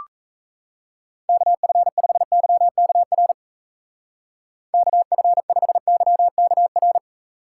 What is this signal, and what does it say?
Morse code, heard as a keyed tone, about 36 wpm, 700 Hz. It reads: KF5YKR KF5YKR